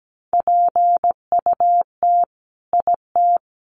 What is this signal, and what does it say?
Morse code, heard as a keyed tone, about 17 wpm, 700 Hz. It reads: PUT IT
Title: Sentences from Top 1400 Words 17wpm